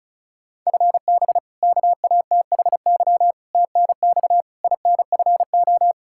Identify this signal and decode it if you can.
Morse code, heard as a keyed tone, about 34 wpm, 700 Hz. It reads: FB KATHY TNX INFO
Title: Practice QSO - Exchange 3 - More Info - 35wpm